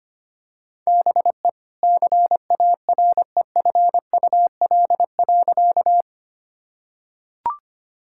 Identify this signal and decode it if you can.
Morse code, heard as a keyed tone, about 25 wpm, 700 Hz. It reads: BE CAREFUL.
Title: Sentences from Top 1600 Words 25wpm